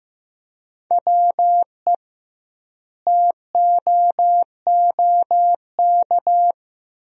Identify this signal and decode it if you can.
Morse code, heard as a keyed tone, about 16 wpm, 700 Hz. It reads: WE TOOK
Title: 2x Word-Spacing - Sentences from Top 400 Words - 15wpm